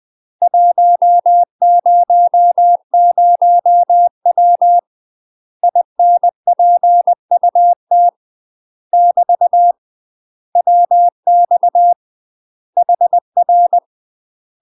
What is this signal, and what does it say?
Morse code, heard as a keyed tone, about 20 wpm, 700 Hz. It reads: 100W INPUT <BT> WX HR
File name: UK MASTER QSOs TRAINING / 20WPM / UK MASTER QSOs 0023